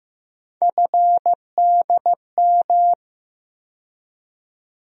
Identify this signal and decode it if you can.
Morse code, heard as a keyed tone, about 15 wpm, 700 Hz. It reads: FDM
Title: USA Call Sign Postfixes - 3 Letters 15wpm